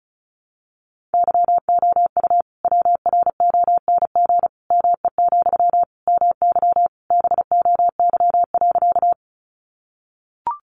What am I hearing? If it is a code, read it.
Morse code, heard as a keyed tone, about 34 wpm, 700 Hz. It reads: YOU WRONG ME, MY BOY.
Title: Sentences from Top 900 Words 35wpm